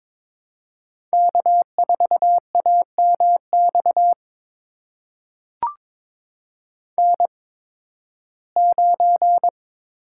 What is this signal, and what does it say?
Morse code, heard as a keyed tone, about 22 wpm, spaced as for 14 wpm, 700 Hz. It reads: K4AMX N9
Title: USA Call Signs - 5 Characters - 8x Character Spacing - 22wpm